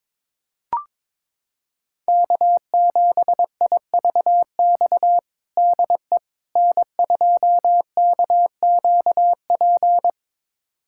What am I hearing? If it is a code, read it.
Morse code, heard as a keyed tone, about 22 wpm, 700 Hz. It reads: K7IVX DE N2KQP
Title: USA Callsign DE Callsign - 5 Characters - 22wpm